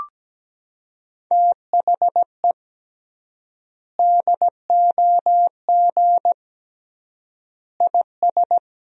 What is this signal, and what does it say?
Morse code, heard as a keyed tone, about 17 wpm, 700 Hz. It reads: THE DOG IS
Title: 3x Word-Spacing - Sentences from Top 400 Words - 17wpm